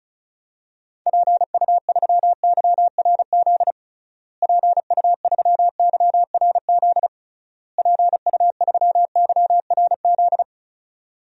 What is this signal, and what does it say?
Morse code, heard as a keyed tone, about 36 wpm, 700 Hz. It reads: PU3YRZ PU3YRZ PU3YRZ
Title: DXCC-100 Call Signs - 6 Characters - Repeated 3x - 35wpm